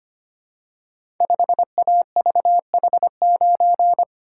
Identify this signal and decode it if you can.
Morse code, heard as a keyed tone, about 25 wpm, 690 Hz. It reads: 5AVH9
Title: Code Groups - 5 Letter:Numbers - 25wpm